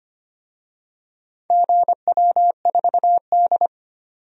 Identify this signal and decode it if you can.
Morse code, heard as a keyed tone, about 25 wpm, 700 Hz. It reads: GW4D
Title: DXCC-100 Call Signs - 4 Characters - 25wpm